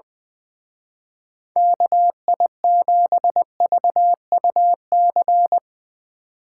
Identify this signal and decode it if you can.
Morse code, heard as a keyed tone, about 22 wpm, 700 Hz. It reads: KI7VUC
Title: Speed Racing - USA 2x3 Call Signs 20wpm